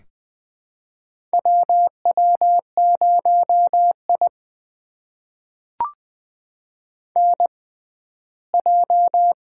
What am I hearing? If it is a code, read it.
Morse code, heard as a keyed tone, about 20 wpm, spaced as for 14 wpm, 700 Hz. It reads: WW0I NJ
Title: USA Call Signs - 4 Characters - 6x Character Spacing - 20wpm